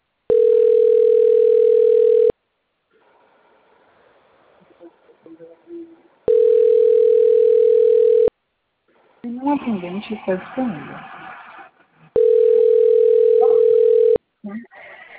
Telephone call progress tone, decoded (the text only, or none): ringback tone